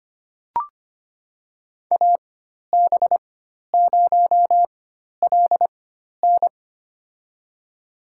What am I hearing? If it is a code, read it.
Morse code, heard as a keyed tone, about 25 wpm, spaced as for 12 wpm, 700 Hz. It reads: AB0LN
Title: USA Call Signs - 5 Characters - 4x Character Spacing - 25wpm